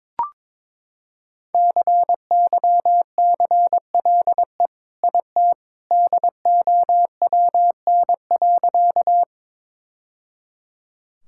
Morse code, heard as a keyed tone, about 22 wpm, 700 Hz. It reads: CYCLE IT DOWN.
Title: Sentences from Top 1000 Words - Review Part 6 - 22wpm